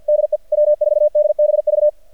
Morse code, heard as a keyed tone, about 51 wpm, 600 Hz, with radio noise. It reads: DE W4DBV
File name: archived / CRSnet